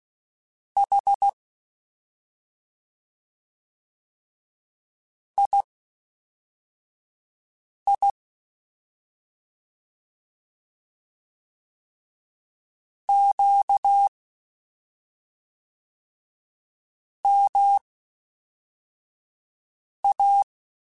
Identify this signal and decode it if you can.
Morse code, heard as a keyed tone, about 16 wpm, spaced as for 3 wpm, 800 Hz. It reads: H II QMA
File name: CW Comparisons / CW Comparisons-mp3s / CW Comparisons Broken Characters @5